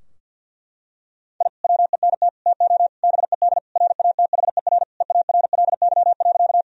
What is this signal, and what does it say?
Morse code, heard as a keyed tone, about 50 wpm, 700 Hz. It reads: I WENT TO BED RATHER EARLY.
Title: Sentences from Top 1000 Words - Review Part 4 - 50wpm